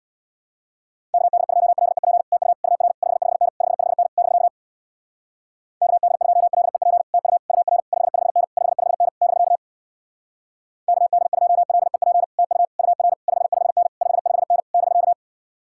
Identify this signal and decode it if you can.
Morse code, heard as a keyed tone, about 66 wpm, 700 Hz. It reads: KD2LEP TU UR 55N 55N <BK> KD2LEP TU UR 55N 55N <BK> KD2LEP TU UR 55N 55N <BK>